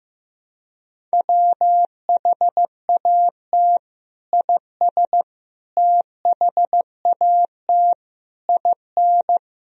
Morse code, heard as a keyed tone, about 15 wpm, 700 Hz. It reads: WHAT IS THAT IN
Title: Sentences from Top 200 Words and Abbreviations 15wpm